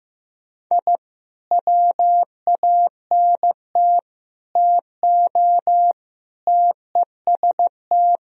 Morse code, heard as a keyed tone, about 15 wpm, 700 Hz. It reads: I WANT TO TEST
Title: Sentences from Top 1000 Words - Review Part 3 - 15wpm